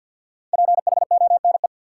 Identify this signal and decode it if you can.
Morse code, heard as a keyed tone, about 52 wpm, 700 Hz. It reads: PHONE
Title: Sentences from Top 1100 Words - New Word Drill Alt - 50wpm